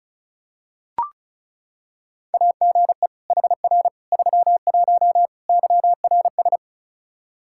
Morse code, heard as a keyed tone, about 36 wpm, 700 Hz. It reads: AGE HR 31 YRS
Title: Practice QSO - Exchange 3 - More Info - 35wpm